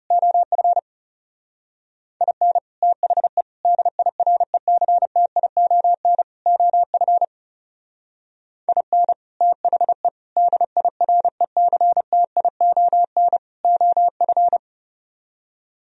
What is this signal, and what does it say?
Morse code, heard as a keyed tone, about 34 wpm, 700 Hz. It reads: OF IN THE DIRECTION OF IN THE DIRECTION OF
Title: Speed-Racing - Sets of 4 Words 30wpm